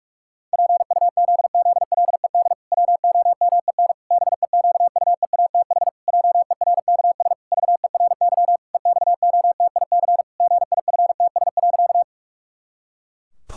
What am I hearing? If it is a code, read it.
Morse code, heard as a keyed tone, about 44 wpm, 700 Hz. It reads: PUZZLED WOMEN BEQUEATH JERKS VERY EXOTIC GIFTS.
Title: Pangram Challenge v2 - 45wpm